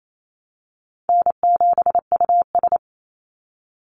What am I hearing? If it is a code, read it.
Morse code, heard as a keyed tone, about 28 wpm, 700 Hz. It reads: N7US
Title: Easy Call Signs - 28wpm